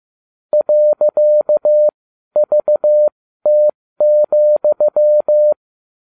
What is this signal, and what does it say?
Morse code, heard as a keyed tone, about 15 wpm, spaced as for 12 wpm, 600 Hz. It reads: .VT,
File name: cw / ubacw30 3